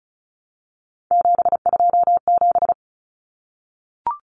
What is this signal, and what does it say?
Morse code, heard as a keyed tone, about 34 wpm, 700 Hz. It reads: 727
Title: Numbers - 3-Digit 35wpm